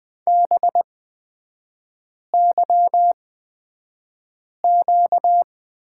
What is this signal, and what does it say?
Morse code, heard as a keyed tone, about 20 wpm, spaced as for 5 wpm, 700 Hz. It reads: BYQ
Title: 236 Lesson 38e - Letter q - Three Characters 20wpm